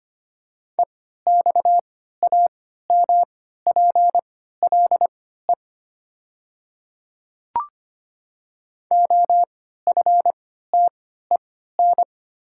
Morse code, heard as a keyed tone, about 25 wpm, spaced as for 14 wpm, 690 Hz. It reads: EXAMPLE OFTEN
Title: 3x Character Spacing - Top 300 Words - 25wpm